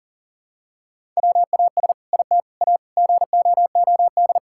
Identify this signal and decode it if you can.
Morse code, heard as a keyed tone, about 40 wpm, 700 Hz. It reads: WAS IT A GOOD
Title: Sentences from Top 1000 Words - Review Part 5 - 40wpm